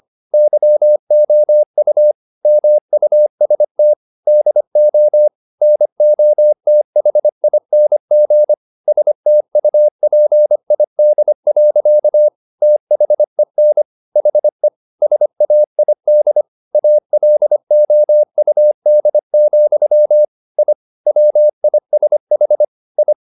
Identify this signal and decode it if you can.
Morse code, heard as a keyed tone, about 25 wpm, 600 Hz. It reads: YOU MUST DO NOTHING STUPID. THEN HE SAID ALOUD, I WISH I
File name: the old man and the sea part 1 25wpm